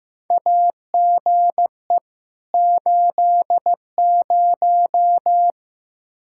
Morse code, heard as a keyed tone, about 15 wpm, 700 Hz. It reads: AGE 80
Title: Practice QSO - Exchange 3 - More Info - 15wpm